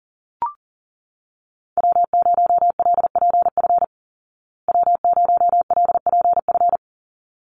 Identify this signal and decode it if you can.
Morse code, heard as a keyed tone, about 40 wpm, 700 Hz. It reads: W0LPF W0LPF
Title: Call Signs Repeated 3x - 40wpm